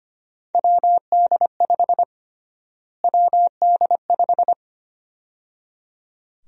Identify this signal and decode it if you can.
Morse code, heard as a keyed tone, about 25 wpm, 700 Hz. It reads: WD5 WD5
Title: USA Call Sign Prefixes - Repeated 2x - 25wpm